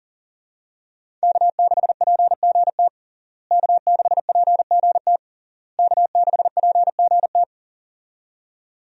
Morse code, heard as a keyed tone, about 40 wpm, 700 Hz. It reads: K6PGT K6PGT K6PGT
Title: USA Call Signs - 5 Characters - Repeated 3x - 40wpm